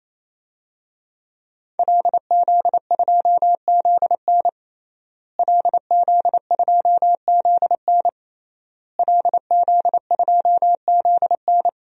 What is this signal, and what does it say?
Morse code, heard as a keyed tone, about 28 wpm, 700 Hz. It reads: LZ2ZN LZ2ZN LZ2ZN
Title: DXCC-100 Call Signs - 5 Characters - Repeated 3x - 28wpm